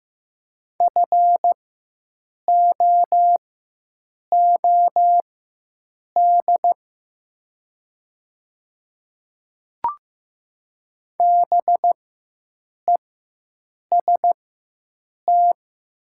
Morse code, heard as a keyed tone, about 15 wpm, spaced as for 7 wpm, 700 Hz. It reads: FOOD BEST
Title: 4x Character Spacing - Top 500 Words - 15wpm